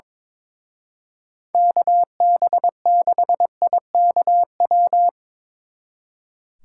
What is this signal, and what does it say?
Morse code, heard as a keyed tone, about 24 wpm, 700 Hz. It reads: KB6IKW
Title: Speed Racing - USA 2x3 Call Signs 22wpm